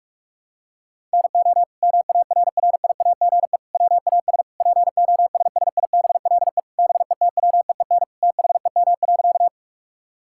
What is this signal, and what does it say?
Morse code, heard as a keyed tone, about 46 wpm, 700 Hz. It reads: NO MARRIAGE WAS POSSIBLE BETWEEN THEM.
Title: Sentences from Top 1000 Words - Review Part 5 - 45wpm